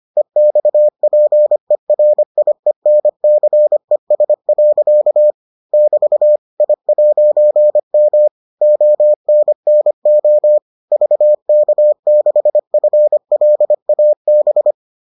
Morse code, heard as a keyed tone, about 25 wpm, 600 Hz. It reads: EXPERIENCES. <BT> I'M ONNO VK6FLAB